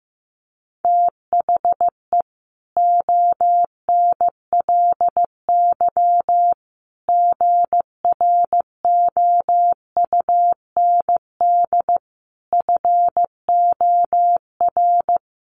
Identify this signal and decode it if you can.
Morse code, heard as a keyed tone, about 15 wpm, 700 Hz. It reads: THE ONLY GROUND FOR
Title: Sentences from Top 1400 Words 15wpm